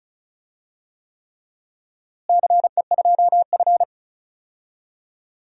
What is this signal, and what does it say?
Morse code, heard as a keyed tone, about 36 wpm, 700 Hz. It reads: CE2F
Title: DXCC-100 Call Signs - 4 Characters - Repeated 3x - 35wpm